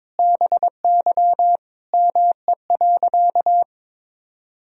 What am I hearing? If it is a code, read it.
Morse code, heard as a keyed tone, about 22 wpm, 700 Hz. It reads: BY ME.
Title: Sentences from Top 500 Words and Abbreviations 22wpm